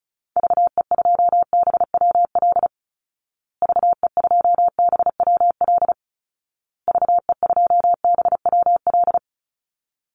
Morse code, heard as a keyed tone, about 34 wpm, 700 Hz. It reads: VE2BWL VE2BWL VE2BWL
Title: Call Signs Repeated 2x - 35wpm